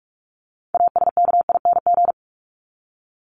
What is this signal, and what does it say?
Morse code, heard as a keyed tone, about 44 wpm, 700 Hz. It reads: ASKING
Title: ING Suffix - Top 100 Words with Base Words 45wpm